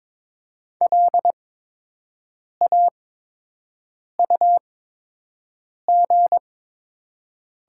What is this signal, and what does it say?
Morse code, heard as a keyed tone, about 22 wpm, spaced as for 6 wpm, 700 Hz. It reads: LAUG
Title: 8x Character Spacing - Top 500 Words - 22wpm